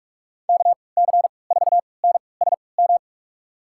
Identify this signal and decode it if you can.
Morse code, heard as a keyed tone, about 46 wpm, spaced as for 26 wpm, 700 Hz. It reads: KC4NSM
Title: USA Call Signs - 6 Characters - 3x Character Spacing - 45wpm